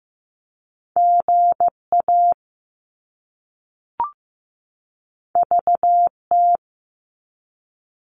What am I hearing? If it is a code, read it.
Morse code, heard as a keyed tone, about 15 wpm, 700 Hz. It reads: GA VT
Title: US States 15wpm